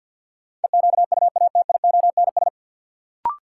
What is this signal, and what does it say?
Morse code, heard as a keyed tone, about 53 wpm, 700 Hz. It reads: EQUATIONS
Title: Sets of 5 Words - New Word Drill - 50wpm